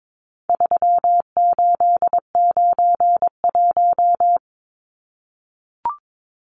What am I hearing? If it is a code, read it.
Morse code, heard as a keyed tone, about 22 wpm, 700 Hz. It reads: 3891
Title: Numbers - 4-Digit 22wpm